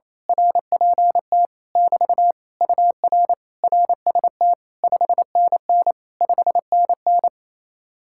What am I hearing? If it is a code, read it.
Morse code, heard as a keyed tone, about 26 wpm, 700 Hz. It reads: RPT <BT> UR RST 5NN 5NN
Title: Speed-Racing Common QSO Phrases Encore - 28wpm